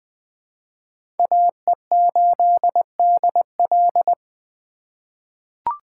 Morse code, heard as a keyed tone, about 21 wpm, 700 Hz. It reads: AE8DL
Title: USA Call Signs - 5 Characters - 3x Character Spacing - 20wpm